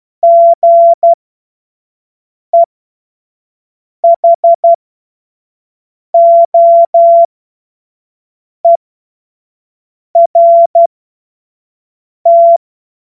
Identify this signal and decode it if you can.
Morse code, heard as a keyed tone, about 12 wpm, spaced as for 5 wpm, 680 Hz. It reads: GEHOERT
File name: Morsekurs / Witze / Witz 07. Juni 2022 25 F70